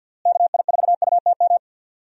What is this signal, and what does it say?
Morse code, heard as a keyed tone, about 46 wpm, 700 Hz. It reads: KI4UTM
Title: USA Call Signs - 6 Characters - Repeated 3x - 50wpm